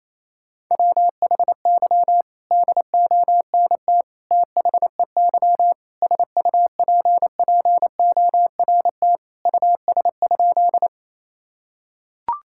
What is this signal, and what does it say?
Morse code, heard as a keyed tone, about 28 wpm, 700 Hz. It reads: WHY DONT THEY SUPPORT US?
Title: Sentences from Top 1000 Words - Review Part 5 - 28wpm